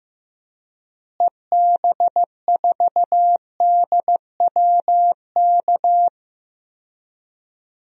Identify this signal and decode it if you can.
Morse code, heard as a keyed tone, about 15 wpm, 700 Hz. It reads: EB4DWK
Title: DXCC-100 Call Signs - 6 Characters - Repeated 3x - 15wpm